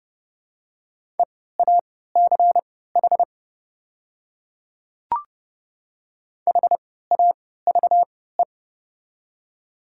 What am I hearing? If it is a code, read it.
Morse code, heard as a keyed tone, about 30 wpm, spaced as for 17 wpm, 690 Hz. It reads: EACH HAVE
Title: 3x Character Spacing - Top 100 Words - 30wpm